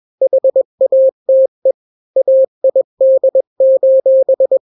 Morse code, heard as a keyed tone, about 21 wpm, 530 Hz. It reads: HATE AID:
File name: daily / 20wpm